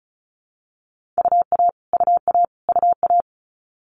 Morse code, heard as a keyed tone, about 36 wpm, 700 Hz. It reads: UA UA UA
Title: Multi-letter Phonemes Repeated 3x 35wpm